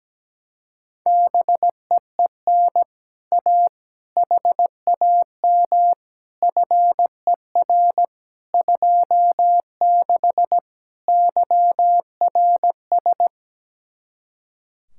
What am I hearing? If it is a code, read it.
Morse code, heard as a keyed tone, about 17 wpm, 700 Hz. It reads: BEEN A HAM FER 26 YRS
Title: Speed-Racing Common QSO Phrases - 17wpm